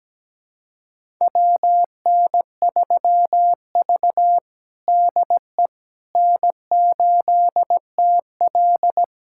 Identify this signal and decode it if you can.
Morse code, heard as a keyed tone, about 17 wpm, 700 Hz. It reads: WN3V DE N8TL
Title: USA Callsign DE Callsign - 4 Characters - 17wpm